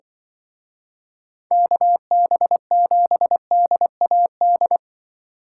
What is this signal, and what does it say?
Morse code, heard as a keyed tone, about 24 wpm, 700 Hz. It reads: KB7DAD
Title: Speed Racing - USA 2x3 Call Signs 20wpm